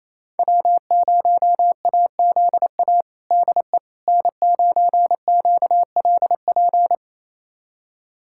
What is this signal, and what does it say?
Morse code, heard as a keyed tone, about 28 wpm, 700 Hz. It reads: W0AZA DE N9QLP
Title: USA Callsign DE Callsign - 5 Characters - 28wpm